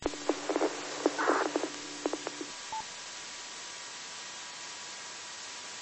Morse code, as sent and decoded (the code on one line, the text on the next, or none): none